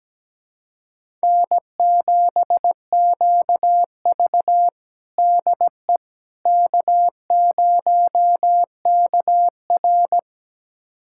Morse code, heard as a keyed tone, about 17 wpm, 700 Hz. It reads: N7QV DE K0KR
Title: USA Callsign DE Callsign - 4 Characters - 17wpm